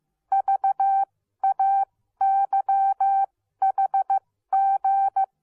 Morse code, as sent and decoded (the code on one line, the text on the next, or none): ...- .- -.-- .... --.
VAYHG